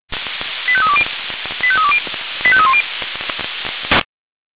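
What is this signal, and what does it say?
Five-tone selective call (ZVEI), heard as quick runs of five tones, tones of about 70 ms. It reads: 85320 85320 85310
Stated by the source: Five-tone ZVEI selective call received by SDR.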